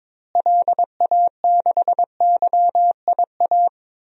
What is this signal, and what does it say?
Morse code, heard as a keyed tone, about 22 wpm, 700 Hz. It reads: LA6YIA
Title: DXCC-100 Call Signs - 6 Characters - Repeated 3x - 22wpm